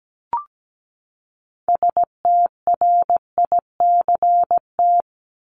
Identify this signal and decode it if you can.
Morse code, heard as a keyed tone, about 17 wpm, 700 Hz. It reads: STRICT
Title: Top 1501-1750 Words 17wpm